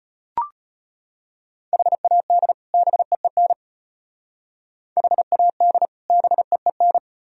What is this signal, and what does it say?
Morse code, heard as a keyed tone, about 31 wpm, 700 Hz. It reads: HAD BEEN HAD BEEN